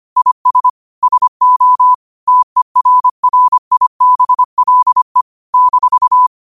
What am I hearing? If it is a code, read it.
Morse code, heard as a keyed tone, about 25 wpm, 1000 Hz. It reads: IS SO TERRIBLE -
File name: qotdfiles / 25wpm